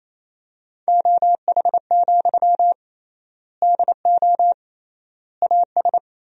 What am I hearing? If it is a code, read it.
Morse code, heard as a keyed tone, about 28 wpm, 700 Hz. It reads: OH, DO AS
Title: 3x Word-Spacing - Sentences from Top 500 Words - 28wpm